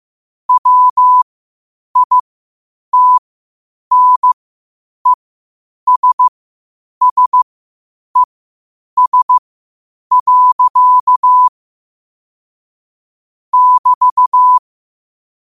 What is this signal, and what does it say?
Morse code, heard as a keyed tone, about 15 wpm, spaced as for 8 wpm, 1000 Hz. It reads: WITNESSES. <BT>
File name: qotdfiles / 8wpm